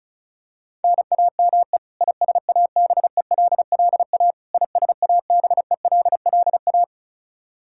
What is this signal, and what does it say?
Morse code, heard as a keyed tone, about 36 wpm, 700 Hz. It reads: NAME ISABELLA ISABELLA
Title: Practice QSO - Exchange 1 - Beginning - 35wpm